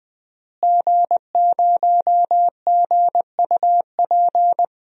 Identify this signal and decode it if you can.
Morse code, heard as a keyed tone, about 21 wpm, 700 Hz. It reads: G0GUP
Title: DXCC-100 Call Signs - 5 Characters - Repeated 3x - 20wpm